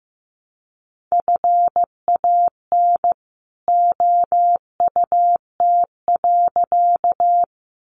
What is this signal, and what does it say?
Morse code, heard as a keyed tone, about 15 wpm, 700 Hz. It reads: FAN OUT.
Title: Sentences from Top 1300 Words 15wpm